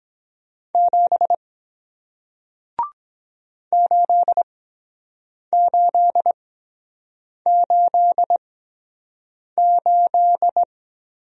Morse code, heard as a keyed tone, about 23 wpm, spaced as for 7 wpm, 700 Hz. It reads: 7 8888